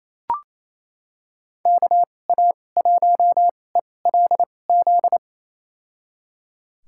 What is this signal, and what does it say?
Morse code, heard as a keyed tone, about 28 wpm, spaced as for 20 wpm, 700 Hz. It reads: KA1ELZ
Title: USA Call Signs - 6 Characters - 2x Character Spacing - 28wpm